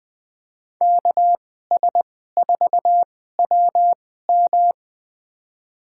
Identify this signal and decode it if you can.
Morse code, heard as a keyed tone, about 20 wpm, spaced as for 14 wpm, 700 Hz. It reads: KS4WM
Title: USA Call Signs - 2x Character Spacing - 20wpm